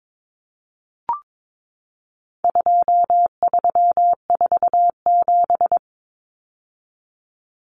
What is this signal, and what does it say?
Morse code, heard as a keyed tone, about 22 wpm, 700 Hz. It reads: 2347
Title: Numbers - 4-Digit 22wpm